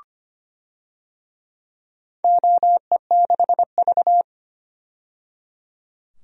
Morse code, heard as a keyed tone, about 25 wpm, 700 Hz. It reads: OE6V
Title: DXCC-100 Call Signs - 4 Characters - 25wpm